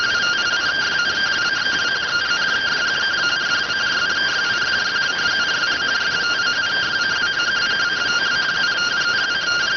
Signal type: two-tone teleprinter signal

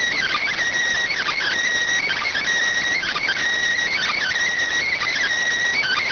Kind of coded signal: Emergency Alert System (SAME) tones

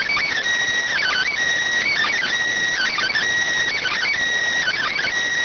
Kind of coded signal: Emergency Alert System (SAME) tones

